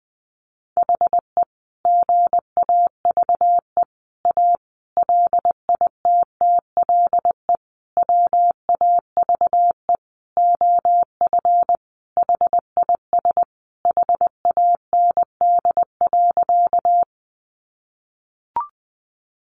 Morse code, heard as a keyed tone, about 20 wpm, 700 Hz. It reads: HE GAVE A LITTLE WAVE OF HIS HAND.
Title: Sentences from Top 1400 Words 20wpm